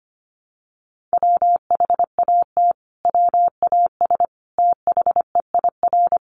Morse code, heard as a keyed tone, about 25 wpm, 700 Hz. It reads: WHAT WAS THEIR
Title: Sentences from Top 1400 Words 25wpm